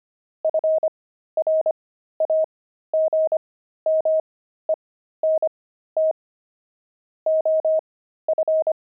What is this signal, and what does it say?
Morse code, heard as a keyed tone, about 25 wpm, spaced as for 13 wpm, 630 Hz. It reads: FRAGMENT OF
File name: pangram-fs-13-cs-25